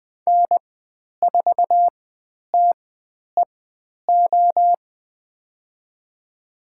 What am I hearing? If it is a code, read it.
Morse code, heard as a keyed tone, about 20 wpm, spaced as for 10 wpm, 700 Hz. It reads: N4TEO
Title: 102 Lesson 17g - Letter u - Call Signs 20wpm